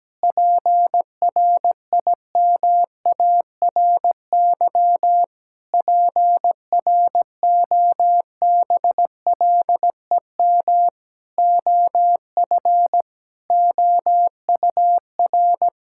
Morse code, heard as a keyed tone, about 17 wpm, 700 Hz. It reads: PRIMARY PROBLEM OF OUR
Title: Sentences from Top 1000 Words - Review Part 5 - 17wpm